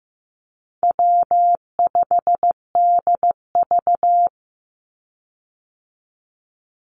Morse code, heard as a keyed tone, about 15 wpm, 700 Hz. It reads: W5DV
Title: Call Signs Repeated 2x - 15wpm